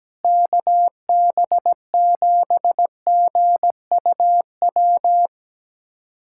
Morse code, heard as a keyed tone, about 17 wpm, 700 Hz. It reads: KB7GUW
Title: USA Callsign DE Callsign - 6 Characters - 17wpm